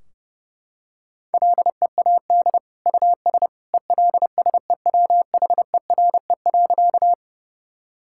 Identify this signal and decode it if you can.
Morse code, heard as a keyed tone, about 30 wpm, 700 Hz. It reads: LEAD US ELSEWHERE.
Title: Sentences from Top 1600 Words 30wpm